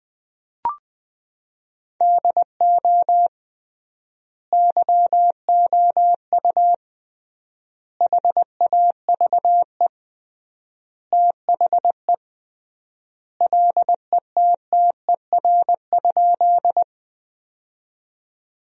Morse code, heard as a keyed tone, about 20 wpm, 700 Hz. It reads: DO YOU HAVE THE LETTER?
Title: 3x Word-Spacing - Sentences from Top 300 Words - 20wpm